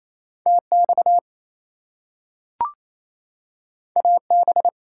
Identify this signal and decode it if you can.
Morse code, heard as a keyed tone, about 28 wpm, 700 Hz. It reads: TX AB
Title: US States and Canadian Provinces 28wpm